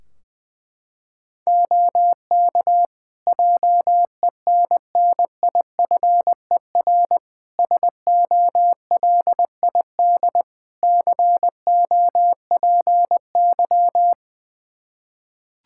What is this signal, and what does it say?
Morse code, heard as a keyed tone, about 21 wpm, 700 Hz. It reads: OK JENNIFER SOLID COPY
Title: Practice QSO - Exchange 3 - More Info - 20wpm